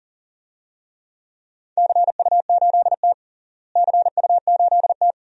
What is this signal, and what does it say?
Morse code, heard as a keyed tone, about 40 wpm, 700 Hz. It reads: CU8T CU8T
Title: DXCC-100 Call Signs - 4 Characters - Repeated 2x - 40wpm